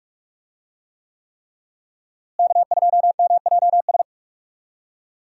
Morse code, heard as a keyed tone, about 46 wpm, 700 Hz. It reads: K2MJS
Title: USA Call Signs - 5 Characters - Repeated 3x - 45wpm